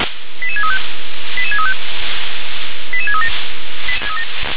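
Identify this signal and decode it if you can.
Five-tone selective call (ZVEI), heard as quick runs of five tones, tones of about 80 ms. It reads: 88635 88635 88637
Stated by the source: ZVEI five-tone selective call received on an SDR.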